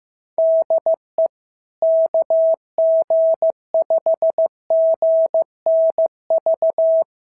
Morse code, heard as a keyed tone, about 14 wpm, 650 Hz. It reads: DE KG5GNV
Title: SOTA QSO Practice 15wpm